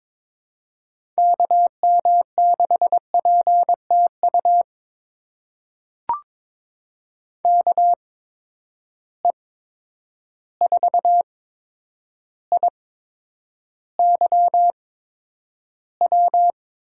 Morse code, heard as a keyed tone, about 22 wpm, spaced as for 9 wpm, 700 Hz. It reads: KM6PTU KE4IYW
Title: USA Call Signs - 6 Characters - 8x Character Spacing - 22wpm